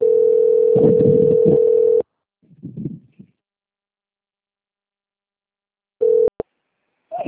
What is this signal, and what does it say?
Ringback tone (North America), 440 + 480 Hz.